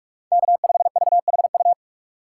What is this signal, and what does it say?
Morse code, heard as a keyed tone, about 46 wpm, 700 Hz. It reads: K5VHU
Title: USA Call Signs - 5 Characters - 6x Character Spacing - 45wpm